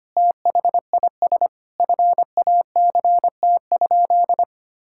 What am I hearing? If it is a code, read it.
Morse code, heard as a keyed tone, about 25 wpm, 700 Hz. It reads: THIS FACT?
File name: practice / 25wpm / Sentences with 2-Word Phrases - 25wpm